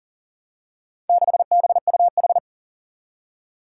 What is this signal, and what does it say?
Morse code, heard as a keyed tone, about 42 wpm, 700 Hz. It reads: 6BUH
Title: Four Challenging Letter-Number Combinations 40wpm